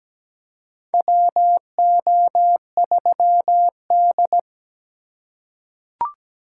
Morse code, heard as a keyed tone, about 17 wpm, 700 Hz. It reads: WO3D
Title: USA Call Signs - 4 Characters - 4x Character Spacing - 17wpm